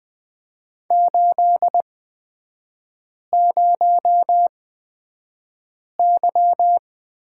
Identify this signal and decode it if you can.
Morse code, heard as a keyed tone, about 20 wpm, spaced as for 5 wpm, 700 Hz. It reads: 80Y
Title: 236 Lesson 38e - Letter q - Three Characters 20wpm